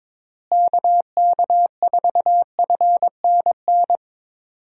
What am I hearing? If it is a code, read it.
Morse code, heard as a keyed tone, about 22 wpm, 700 Hz. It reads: KK4FNN
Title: USA Call Signs - 6 Characters - Repeated 2x - 22wpm